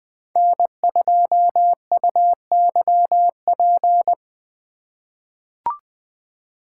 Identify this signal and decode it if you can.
Morse code, heard as a keyed tone, about 21 wpm, 700 Hz. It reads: N2UYP
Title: USA Call Signs - 5 Characters - 4x Character Spacing - 20wpm